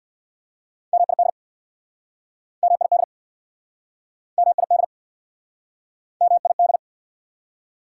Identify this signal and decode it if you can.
Morse code, heard as a keyed tone, about 69 wpm, 690 Hz. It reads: MID MID MID MID